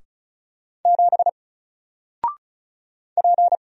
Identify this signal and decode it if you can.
Morse code, heard as a keyed tone, about 35 wpm, 700 Hz. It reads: 7 P
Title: Single Character (Letter:Number:Punctuation:Prosign) 35wpm